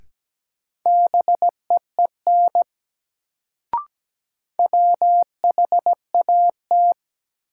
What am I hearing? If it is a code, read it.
Morse code, heard as a keyed tone, about 17 wpm, 700 Hz. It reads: BEEN WHAT
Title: Sentences from Top 100 Words - New Word Drill - 17wpm